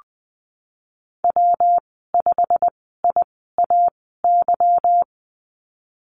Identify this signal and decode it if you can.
Morse code, heard as a keyed tone, about 19 wpm, spaced as for 14 wpm, 700 Hz. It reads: W5IAY
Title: 193 Lesson 31g - Letter v - Call Signs 20wpm